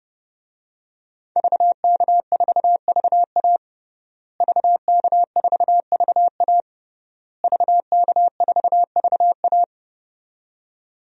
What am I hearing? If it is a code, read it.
Morse code, heard as a keyed tone, about 30 wpm, 700 Hz. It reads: VK4VA VK4VA VK4VA
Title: DXCC-100 Call Signs - 5 Characters - Repeated 3x - 30wpm